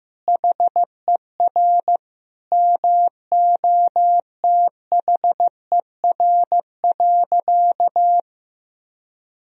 Morse code, heard as a keyed tone, about 15 wpm, 700 Hz. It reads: HER MOTHER.
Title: Sentences from Top 1000 Words - Review Part 4 - 15wpm